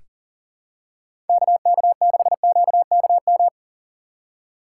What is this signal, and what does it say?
Morse code, heard as a keyed tone, about 37 wpm, 700 Hz. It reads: KK6QKM
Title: USA Call Signs - 6 Characters - 2x Character Spacing - 40wpm